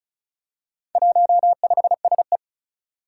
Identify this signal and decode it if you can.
Morse code, heard as a keyed tone, about 36 wpm, 700 Hz. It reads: 15SE
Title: Four Challenging Letter-Number Combinations 35wpm